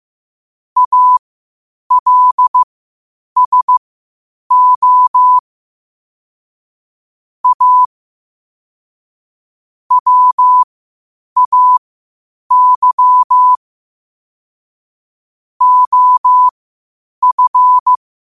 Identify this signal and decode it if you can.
Morse code, heard as a keyed tone, about 15 wpm, spaced as for 8 wpm, 1000 Hz. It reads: ALSO A WAY OF
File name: qotdfiles / 8wpm